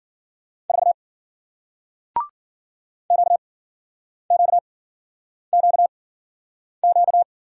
Morse code, heard as a keyed tone, about 52 wpm, spaced as for 9 wpm, 700 Hz. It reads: 4 QQQQ